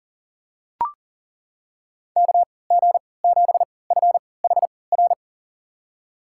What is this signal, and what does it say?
Morse code, heard as a keyed tone, about 40 wpm, spaced as for 23 wpm, 700 Hz. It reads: KG7FHR
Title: USA Call Signs - 6 Characters - 3x Character Spacing - 40wpm